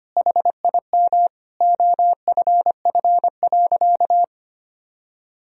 Morse code, heard as a keyed tone, about 25 wpm, 700 Hz. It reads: HIM OFF.
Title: Sentences from Top 1000 Words - Review Part 2 - 25wpm